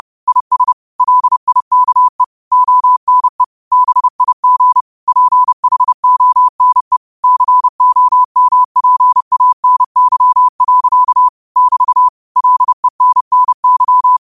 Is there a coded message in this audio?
Yes: Morse code from 0.3 to 14.2 s, heard as a keyed tone, about 30 wpm, 1000 Hz. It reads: IS LIKE ONE BIG PHONE COMPANY. <BT> LENNY